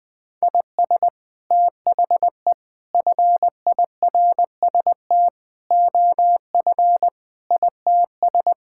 Morse code, heard as a keyed tone, about 20 wpm, 700 Hz. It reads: IS THE FIRST OF ITS
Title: Sets of 8 Words - 20wpm